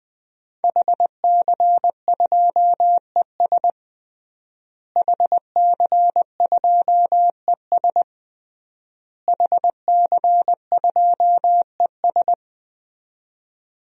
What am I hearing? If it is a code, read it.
Morse code, heard as a keyed tone, about 20 wpm, 700 Hz. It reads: HC2ES HC2ES HC2ES
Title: DXCC-100 Call Signs - 5 Characters - Repeated 3x - 20wpm